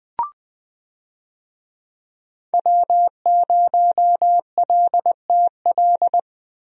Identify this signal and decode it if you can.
Morse code, heard as a keyed tone, about 21 wpm, 700 Hz. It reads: W0LTL
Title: USA Call Signs - 5 Characters - Repeated 3x - 20wpm